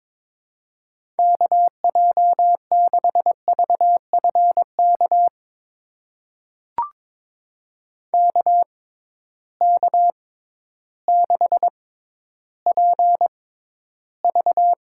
Morse code, heard as a keyed tone, about 22 wpm, spaced as for 12 wpm, 700 Hz. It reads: KJ6VFK KK6PV
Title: USA Call Signs - 6 Characters - 6x Character Spacing - 22wpm